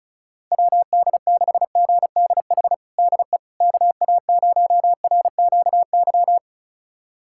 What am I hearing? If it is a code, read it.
Morse code, heard as a keyed tone, about 36 wpm, 700 Hz. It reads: WD6GDH DE KA0RQY
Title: USA Callsign DE Callsign - 6 Characters - 35wpm